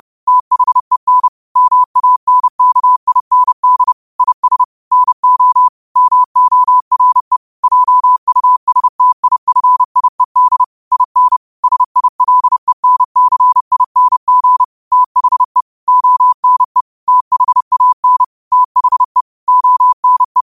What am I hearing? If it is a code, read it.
Morse code, heard as a keyed tone, about 30 wpm, 1000 Hz. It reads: THEN MANKIND IS NO MORE JUSTIFIED IN SILENCING THE ONE THAN THE ONE
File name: qotdfiles / 30wpm